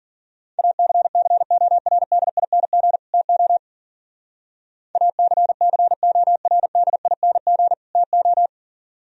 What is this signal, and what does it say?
Morse code, heard as a keyed tone, about 48 wpm, 700 Hz. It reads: ACCORDING TO ACCORDING TO
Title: Speed-Racing - Sets of 2 Words 40wpm